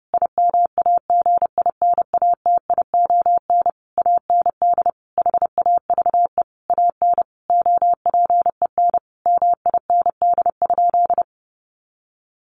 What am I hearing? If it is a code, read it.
Morse code, heard as a keyed tone, about 28 wpm, 700 Hz. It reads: IMAGINATION AND HAVE AN OPEN MIND?
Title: Sentences from Top 1300 Words 30wpm